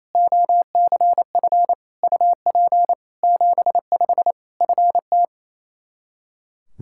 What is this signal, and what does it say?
Morse code, heard as a keyed tone, about 28 wpm, 700 Hz. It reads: OCF UP 75 FT
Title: K1USN SST QSO Practice 28wpm